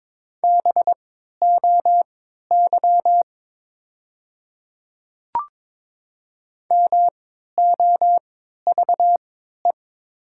Morse code, heard as a keyed tone, about 22 wpm, spaced as for 12 wpm, 690 Hz. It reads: BOY MOVE
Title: 3x Character Spacing - Top 200 Words - 22wpm